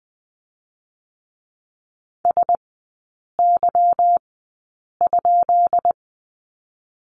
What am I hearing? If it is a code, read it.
Morse code, heard as a keyed tone, about 20 wpm, spaced as for 8 wpm, 700 Hz. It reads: SY?
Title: 218 Lesson 35e - Number 0 - Three Characters 20wpm